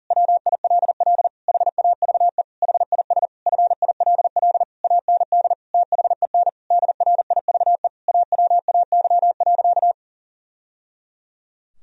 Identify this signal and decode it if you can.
Morse code, heard as a keyed tone, about 40 wpm, 700 Hz. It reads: WILL HAVE HIS FILL AND THEN DRIVE AWAY.
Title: Sentences from Top 1000 Words - Review Part 3 - 40wpm